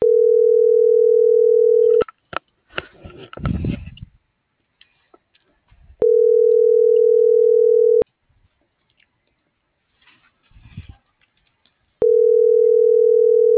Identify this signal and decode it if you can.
Ringback tone (North America), 440 + 480 Hz.